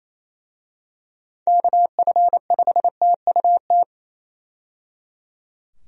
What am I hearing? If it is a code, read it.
Morse code, heard as a keyed tone, about 28 wpm, 700 Hz. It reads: KF5TUT
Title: USA Call Signs - 6 Characters - 28wpm